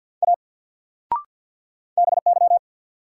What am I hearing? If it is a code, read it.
Morse code, heard as a keyed tone, about 51 wpm, 700 Hz. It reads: A BY
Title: Sentences from Top 100 Words - New Word Drill - 50wpm